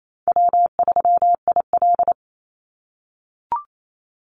Morse code, heard as a keyed tone, about 28 wpm, 700 Hz. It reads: W3IL
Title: Easy Call Signs - 28wpm